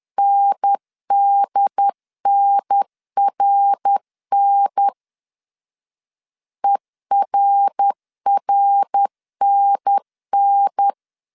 Morse code, heard as a keyed tone, about 10 wpm, 800 Hz. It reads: NDNRN ERRNN
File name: CW / DET / 1 3 1500C S B 10wpm 3335